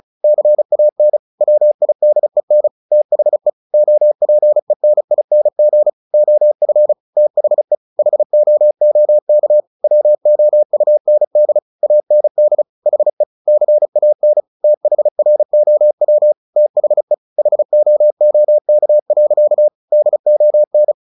Morse code, heard as a keyed tone, about 34 wpm, 600 Hz. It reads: CAN WIDEN THE OPENING OF THE HOOK WOUND AND HE CAN THROW THE HOOK. DON
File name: the old man and the sea 2 35wpm